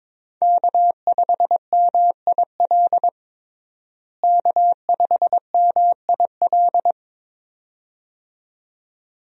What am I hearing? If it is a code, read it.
Morse code, heard as a keyed tone, about 21 wpm, 700 Hz. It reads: K5MIL K5MIL
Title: USA Call Signs - 5 Characters - Repeated 2x - 22wpm